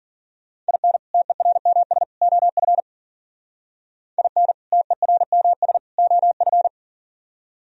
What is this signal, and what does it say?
Morse code, heard as a keyed tone, about 52 wpm, 700 Hz. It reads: IN TERMS OF IN TERMS OF